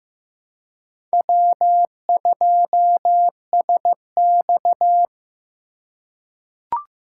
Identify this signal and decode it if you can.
Morse code, heard as a keyed tone, about 16 wpm, 700 Hz. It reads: W2SX
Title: USA Call Signs - 4x Character Spacing - 15wpm